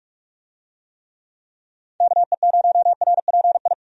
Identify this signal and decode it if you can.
Morse code, heard as a keyed tone, about 44 wpm, 700 Hz. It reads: KE0RPI
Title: USA Call Signs - 6 Characters - 45wpm